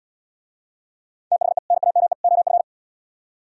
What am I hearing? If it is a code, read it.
Morse code, heard as a keyed tone, about 74 wpm, 700 Hz. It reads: THE NAME OF